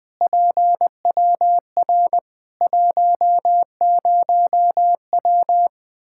Morse code, heard as a keyed tone, about 20 wpm, 700 Hz. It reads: PWR 10W
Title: Practice QSO - Exchange 2 - Info - 20wpm